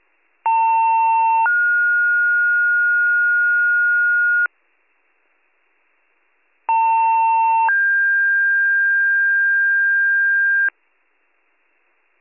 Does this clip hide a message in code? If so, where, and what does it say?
Yes: two-tone page from 0.4 to 10.7 s, heard as a one-second tone, then a three-second tone, about 900 then 1430 Hz, about 900 then 1600 Hz.